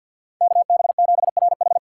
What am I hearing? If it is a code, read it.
Morse code, heard as a keyed tone, about 48 wpm, 700 Hz. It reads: KB7RH
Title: USA Call Signs - 5 Characters - 3x Character Spacing - 50wpm